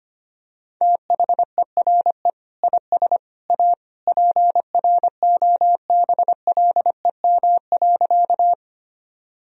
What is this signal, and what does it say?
Morse code, heard as a keyed tone, about 25 wpm, 700 Hz. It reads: THERE IS A PROBLEM.
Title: Sentences from Top 1000 Words - Review Part 2 - 25wpm